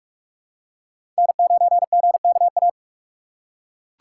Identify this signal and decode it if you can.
Morse code, heard as a keyed tone, about 41 wpm, 700 Hz. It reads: N9GKA
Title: USA Call Signs - 4x Character Spacing - 45wpm